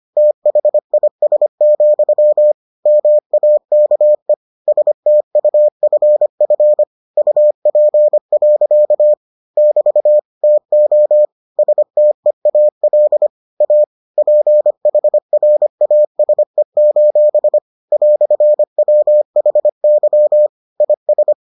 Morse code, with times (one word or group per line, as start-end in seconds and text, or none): 0.2-2.5 THIS,
2.9-4.3 MAKE
4.7-6.8 STUFF
7.2-9.1 UP.
9.6-10.2 <BT>
10.4-11.2 TO
11.6-13.3 STEAL
13.6-13.8 A
14.2-17.6 PHRASE:
17.9-20.5 "WHY
20.8-21.3 IS